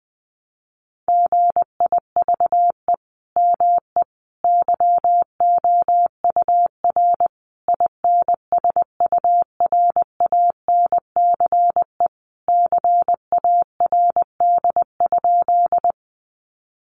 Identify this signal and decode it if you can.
Morse code, heard as a keyed tone, about 20 wpm, 700 Hz. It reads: GIVE ME YOUR INSURANCE CARD?
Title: Sentences from Top 1100 Words 20wpm